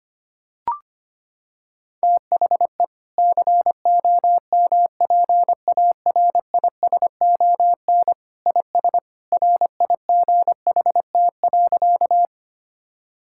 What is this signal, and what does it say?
Morse code, heard as a keyed tone, about 25 wpm, 700 Hz. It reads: THE COMPARISON IS RIGHT.
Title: Sentences from Top 1600 Words 25wpm